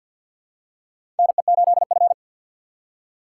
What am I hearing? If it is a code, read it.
Morse code, heard as a keyed tone, about 51 wpm, 700 Hz. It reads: NE8F
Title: USA Call Signs - 4 Characters - 3x Character Spacing - 50wpm